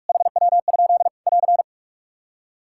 Morse code, heard as a keyed tone, about 46 wpm, 700 Hz. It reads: HW? <AR>
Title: Practice QSO - Exchange 2 - Info - 45wpm